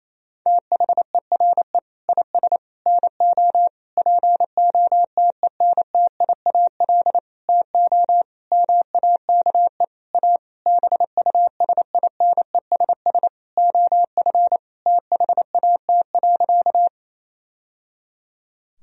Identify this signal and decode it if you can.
Morse code, heard as a keyed tone, about 28 wpm, 700 Hz. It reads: THERE IS NO POTENTIAL TO MAKE A BUSINESS OF THAT.
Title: Sentences from Top 1000 Words - Review Part 4 - 28wpm